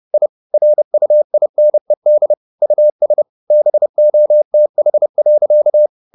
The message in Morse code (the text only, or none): I RUINED US BOTH.